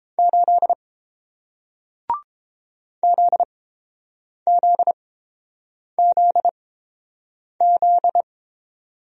Morse code, heard as a keyed tone, about 31 wpm, spaced as for 7 wpm, 700 Hz. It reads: 8 ZZZZ